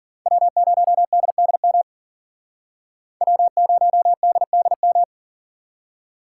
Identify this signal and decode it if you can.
Morse code, heard as a keyed tone, about 52 wpm, 700 Hz. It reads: W0DDM W0DDM